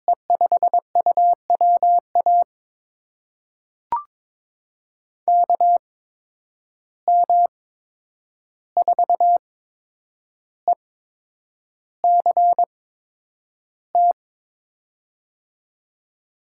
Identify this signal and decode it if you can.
Morse code, heard as a keyed tone, about 22 wpm, spaced as for 9 wpm, 700 Hz. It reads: E5UWA KM4ECT
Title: USA Call Signs - 6 Characters - 8x Character Spacing - 22wpm